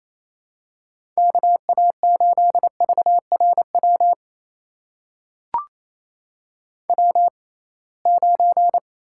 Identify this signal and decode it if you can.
Morse code, heard as a keyed tone, about 28 wpm, spaced as for 21 wpm, 700 Hz. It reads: KA8VRW W9
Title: USA Call Signs - 6x Character Spacing - 28wpm